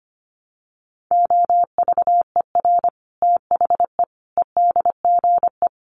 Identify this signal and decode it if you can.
Morse code, heard as a keyed tone, about 25 wpm, 700 Hz. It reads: OVER THE EDGE
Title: Sets of 5 Words Encore2 25wpm